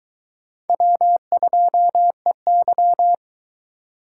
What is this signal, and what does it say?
Morse code, heard as a keyed tone, about 22 wpm, 700 Hz. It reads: W2EY